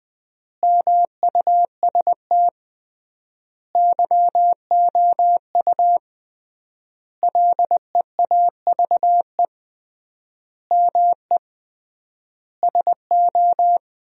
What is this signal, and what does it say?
Morse code, heard as a keyed tone, about 20 wpm, 700 Hz. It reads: MUST YOU LEAVE ME SO
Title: 3x Word-Spacing - Sentences from Top 400 Words - 20wpm